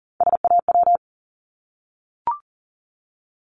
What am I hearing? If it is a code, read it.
Morse code, heard as a keyed tone, about 40 wpm, 700 Hz. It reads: SAW
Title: Multi-letter Phonemes 40wpm